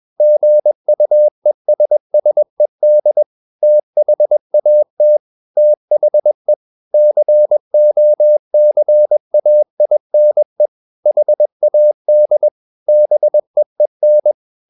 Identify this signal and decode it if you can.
Morse code, heard as a keyed tone, about 21 wpm, 600 Hz. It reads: GUESSED THAT THE COCAINE HAD BEEN
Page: CW med Gnister